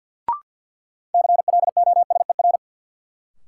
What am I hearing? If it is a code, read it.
Morse code, heard as a keyed tone, about 48 wpm, 700 Hz. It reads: CLOSER
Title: Sets of 5 Words Encore - New Word Drill - 50wpm